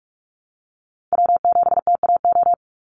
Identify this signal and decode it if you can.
Morse code, heard as a keyed tone, about 44 wpm, 700 Hz. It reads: W7TAO
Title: NA SOTA Chaser Callsigns 45wpm